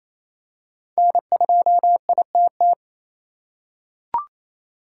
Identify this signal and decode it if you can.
Morse code, heard as a keyed tone, about 27 wpm, 700 Hz. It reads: N2ITT
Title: USA Call Signs - 5 Characters - 4x Character Spacing - 28wpm